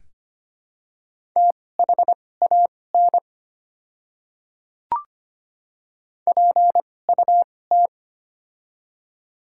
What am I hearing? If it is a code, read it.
Morse code, heard as a keyed tone, about 25 wpm, spaced as for 18 wpm, 700 Hz. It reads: THAN PUT
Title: 2x Character Spacing - Top 300 Words - 25wpm